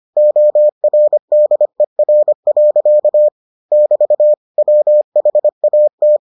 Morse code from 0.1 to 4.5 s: ORDER. <BT>

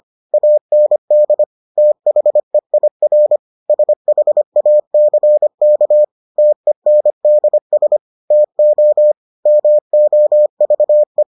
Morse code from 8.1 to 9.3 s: TO

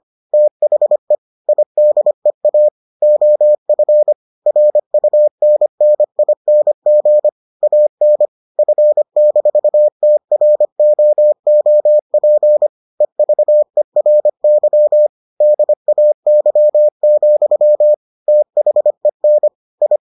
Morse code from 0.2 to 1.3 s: THE

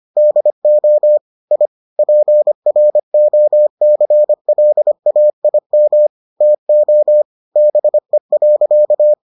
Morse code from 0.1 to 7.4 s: DO I PROCLAIM TO

